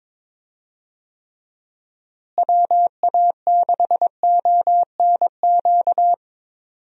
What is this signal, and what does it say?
Morse code, heard as a keyed tone, about 22 wpm, 700 Hz. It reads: WA6ONQ
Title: USA Call Signs - 6 Characters - Repeated 2x - 22wpm